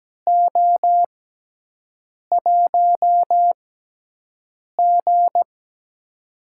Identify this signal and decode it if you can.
Morse code, heard as a keyed tone, about 17 wpm, spaced as for 6 wpm, 700 Hz. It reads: O1G
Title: USA Call Signs - 4 Characters - 6x Character Spacing - 17wpm